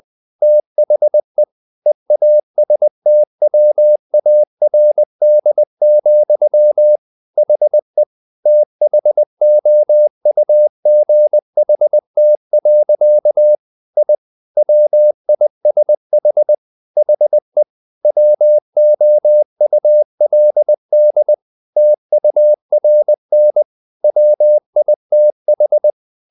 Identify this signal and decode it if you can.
Morse code, heard as a keyed tone, about 20 wpm, 600 Hz. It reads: THE EASTWARD, HE THOUGHT. I WISH HE WOULD TURN WITH
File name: the old man and the sea / the old man and the sea part 1 20wpm